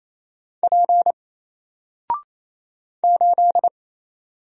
Morse code, heard as a keyed tone, about 28 wpm, 700 Hz. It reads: P 8
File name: Single-Letter-Number-28wpm